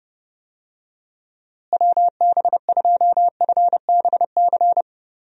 Morse code, heard as a keyed tone, about 32 wpm, 700 Hz. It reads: WB2FBC
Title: USA Call Signs - 6 Characters - Repeated 3x - 30wpm